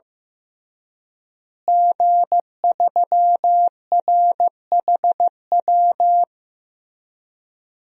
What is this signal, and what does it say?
Morse code, heard as a keyed tone, about 15 wpm, 700 Hz. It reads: G3RHW
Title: DXCC-100 Call Signs - 5 Characters - Repeated 3x - 15wpm